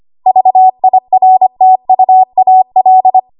Morse code, heard as a keyed tone, about 25 wpm, 750 Hz. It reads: VIRTUAL
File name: Archive / 25 WPM / 25WPM